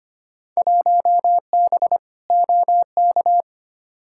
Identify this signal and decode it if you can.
Morse code, heard as a keyed tone, about 25 wpm, 700 Hz. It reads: 1B OK
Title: Field Day Exchanges Encore - 25wpm